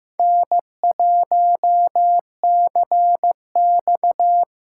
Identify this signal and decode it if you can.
Morse code, heard as a keyed tone, about 15 wpm, 700 Hz. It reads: N1CX
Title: USA Callsign DE Callsign - 4 Characters - 15wpm